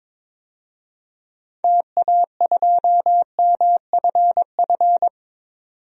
Morse code, heard as a keyed tone, about 21 wpm, 700 Hz. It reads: TA2MFF
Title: DXCC-100 Call Signs - 6 Characters - 22wpm